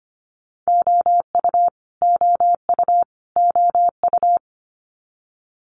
Morse code, heard as a keyed tone, about 25 wpm, 700 Hz. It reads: OU OU OU
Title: Multi-letter Phonemes Repeated 3x 25wpm